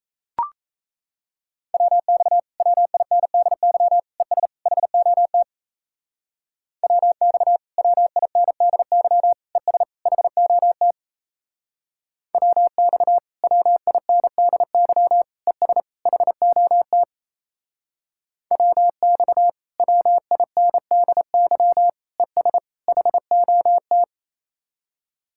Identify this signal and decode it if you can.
Morse code, heard as a keyed tone, about 36 wpm, 700 Hz. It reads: WX WINDY ES HOT WX WINDY ES HOT WX WINDY ES HOT WX WINDY ES HOT